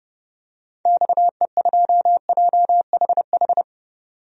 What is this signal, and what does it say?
Morse code, heard as a keyed tone, about 32 wpm, 700 Hz. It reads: XE2JHH
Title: DXCC-100 Call Signs - 6 Characters - 30wpm